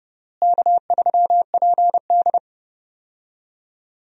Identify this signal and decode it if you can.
Morse code, heard as a keyed tone, about 30 wpm, 700 Hz. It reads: K3PD
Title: USA Call Signs - 4 Characters - Repeated 3x - 30wpm